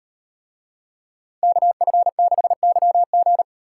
Morse code, heard as a keyed tone, about 38 wpm, 700 Hz. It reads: KF6YG